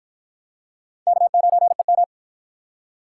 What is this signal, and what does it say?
Morse code, heard as a keyed tone, about 52 wpm, 700 Hz. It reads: K9EM